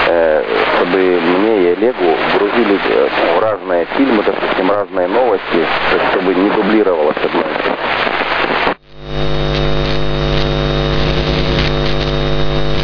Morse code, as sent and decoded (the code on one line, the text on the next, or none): none